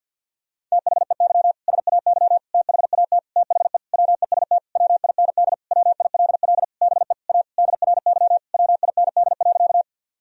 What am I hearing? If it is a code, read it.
Morse code, heard as a keyed tone, about 50 wpm, 700 Hz. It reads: THEY SAY THAT THE WEST WIND WILL BE A DRY WIND.
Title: Sentences from Top 1000 Words - Review Part 3 - 50wpm